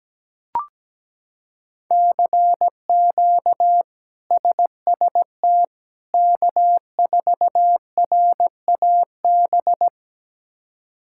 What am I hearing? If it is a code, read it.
Morse code, heard as a keyed tone, about 17 wpm, 700 Hz. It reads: CQ SST K4RAB
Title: K1USN SST QSO Practice - 17wpm